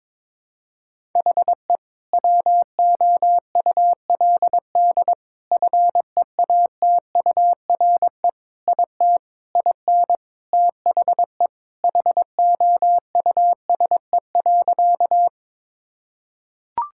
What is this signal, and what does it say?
Morse code, heard as a keyed tone, about 21 wpm, 700 Hz. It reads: HE WOULD FEATURE IT IN THE HOUSE.
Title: Sentences from Top 1000 Words - Review Part 5 - 22wpm